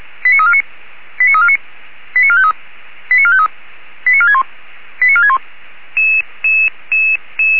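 Five-tone selective call (ZVEI), heard as quick runs of five tones, tones of about 70 ms. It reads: 89349 89349 89453 89453 89461 89461
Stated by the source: ZVEI five-tone call 89609 received by SDR.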